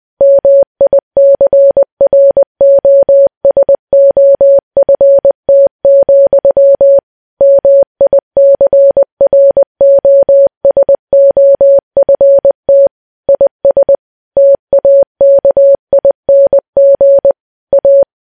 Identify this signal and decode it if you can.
Morse code, heard as a keyed tone, about 20 wpm, 560 Hz. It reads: MICROSOFT, MICROSOFT IS TAKING A
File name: morse